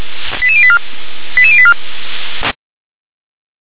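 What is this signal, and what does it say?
Five-tone selective call (ZVEI), heard as quick runs of five tones, tones of about 70 ms. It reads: 88084 88084
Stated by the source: ZVEI five-tone selective call received via SDR.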